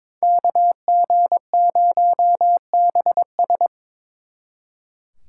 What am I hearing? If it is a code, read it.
Morse code, heard as a keyed tone, about 26 wpm, 700 Hz. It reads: KG0BS
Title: Speed Racing - USA 2x2 Call Signs 22wpm